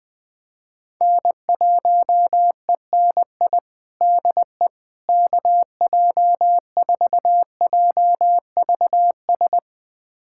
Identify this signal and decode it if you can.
Morse code, heard as a keyed tone, about 20 wpm, 700 Hz. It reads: N1ENI DE KJ4JVS
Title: USA Callsign DE Callsign 20wpm